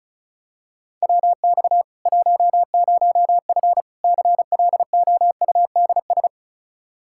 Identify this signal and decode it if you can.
Morse code, heard as a keyed tone, about 36 wpm, 700 Hz. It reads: WX 10F CLOUDS
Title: Practice QSO - Exchange 2 - Info - 35wpm